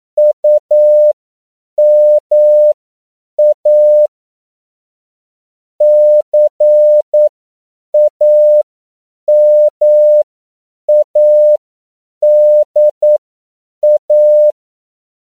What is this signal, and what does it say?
Morse code, heard as a keyed tone, about 9 wpm, spaced as for 7 wpm, 600 Hz. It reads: UMA CAMADA
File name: CW / 10PPM